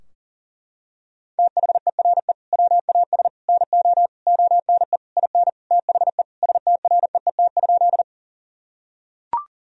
Morse code, heard as a keyed tone, about 40 wpm, 700 Hz. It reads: THERE WAS NO ONE IN THE STREET?
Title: Sentences from Top 1000 Words - Review Part 3 - 40wpm